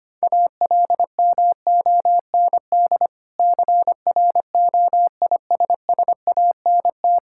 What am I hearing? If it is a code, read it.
Morse code, heard as a keyed tone, about 25 wpm, 690 Hz. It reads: ALMOND CROISSANT
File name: practice / 25wpm / Difficult Words with Hints - 25wpm